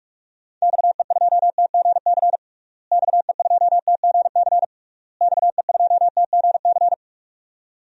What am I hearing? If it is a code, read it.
Morse code, heard as a keyed tone, about 41 wpm, 700 Hz. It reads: XE2TGC XE2TGC XE2TGC
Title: DXCC-100 Call Signs - 6 Characters - Repeated 3x - 45wpm